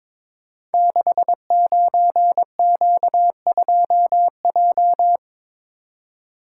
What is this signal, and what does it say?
Morse code, heard as a keyed tone, about 23 wpm, 700 Hz. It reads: 69Q2J
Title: Code Groups - 5 Letter:Numbers - 22wpm